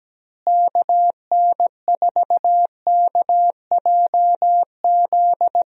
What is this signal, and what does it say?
Morse code, heard as a keyed tone, about 17 wpm, 700 Hz. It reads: KN4KJZ
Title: USA Call Signs - 6 Characters - Repeated 2x - 17wpm